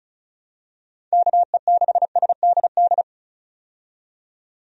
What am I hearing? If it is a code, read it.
Morse code, heard as a keyed tone, about 36 wpm, 700 Hz. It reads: KE6SDD
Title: USA Call Signs - 6 Characters - 35wpm